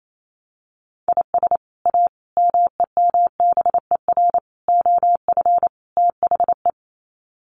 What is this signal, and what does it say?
Morse code, heard as a keyed tone, about 28 wpm, 700 Hz. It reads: IS A MEMBER OF THE
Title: Sets of 5 Words 28wpm